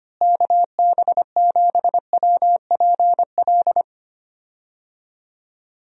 Morse code, heard as a keyed tone, about 25 wpm, 700 Hz. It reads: KB7WPL
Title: USA Call Signs - 6 Characters - Repeated 2x - 25wpm